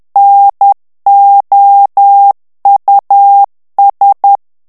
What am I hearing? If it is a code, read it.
Morse code, heard as a keyed tone, about 10 wpm, 800 Hz. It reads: NOUS
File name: FRANCAIS TEXTE CLAIR 2 VITESSE 10 MOTS MINUTES